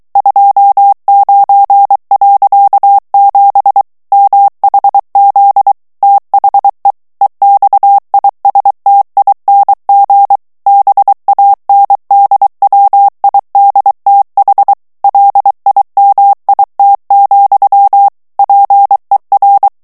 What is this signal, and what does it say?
Morse code, heard as a keyed tone, about 23 wpm, 800 Hz. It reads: 29.7 MHZ THE EXISTING BANDWIDTH LIMIT, PER
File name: ANGLAIS texte clair 3 vitesse 22 mots minute